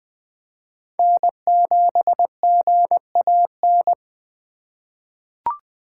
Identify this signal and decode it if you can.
Morse code, heard as a keyed tone, about 20 wpm, 700 Hz. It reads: N7GAN
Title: USA Call Signs - 5 Characters - 2x Character Spacing - 20wpm